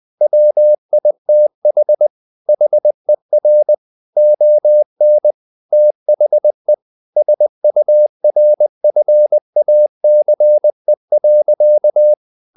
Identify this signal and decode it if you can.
Morse code, heard as a keyed tone, about 20 wpm, 600 Hz. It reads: WITH HER ON THE SURFACE.
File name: the old man and the sea / the old man and the sea part 1 20wpm